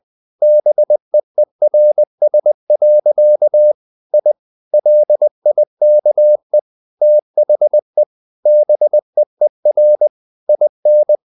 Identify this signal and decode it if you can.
Morse code, heard as a keyed tone, about 20 wpm, 600 Hz. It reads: BEERS. I LIKE THE BEER IN
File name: the old man and the sea / the old man and the sea part 1 20wpm